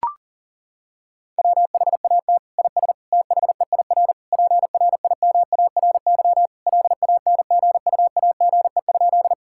Morse code, heard as a keyed tone, about 40 wpm, 700 Hz. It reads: WHAT IS THEIR PRIMARY LANGUAGE?
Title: Sentences from Top 1000 Words - Review Part 5 - 40wpm